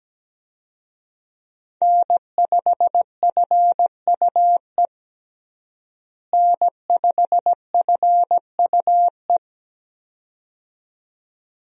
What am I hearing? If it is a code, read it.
Morse code, heard as a keyed tone, about 17 wpm, 700 Hz. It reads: N5FUE N5FUE
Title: USA Call Signs - 5 Characters - Repeated 2x - 17wpm